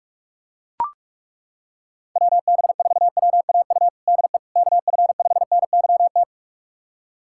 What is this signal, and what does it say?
Morse code, heard as a keyed tone, about 41 wpm, 700 Hz. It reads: WB4WAU DE KF5NYT
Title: USA Callsign DE Callsign - 6 Characters - 45wpm